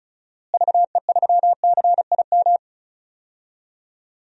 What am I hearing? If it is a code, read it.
Morse code, heard as a keyed tone, about 36 wpm, 700 Hz. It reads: VE3CIM
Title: DXCC-100 Call Signs - 6 Characters - Repeated 2x - 35wpm